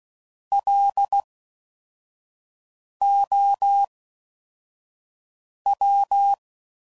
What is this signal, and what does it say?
Morse code, heard as a keyed tone, about 16 wpm, 800 Hz. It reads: L O W
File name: Suffixes / Special INGs / Fast INGs